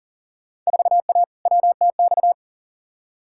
Morse code, heard as a keyed tone, about 40 wpm, 700 Hz. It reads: 4A WTX
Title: Field Day Exchanges Encore - 40wpm